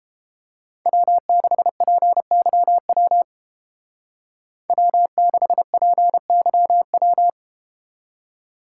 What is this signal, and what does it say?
Morse code, heard as a keyed tone, about 30 wpm, 700 Hz. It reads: W6PYW W6PYW